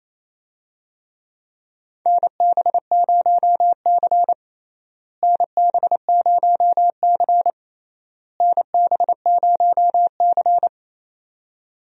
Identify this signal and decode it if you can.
Morse code, heard as a keyed tone, about 27 wpm, 700 Hz. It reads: NB0C NB0C NB0C
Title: USA Call Signs - 4 Characters - Repeated 3x - 28wpm